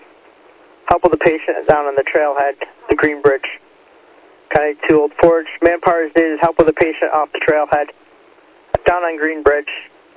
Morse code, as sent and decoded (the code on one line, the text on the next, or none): none